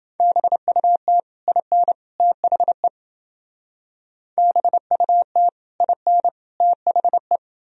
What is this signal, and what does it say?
Morse code, heard as a keyed tone, about 29 wpm, 700 Hz. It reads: BUT IN THE BUT IN THE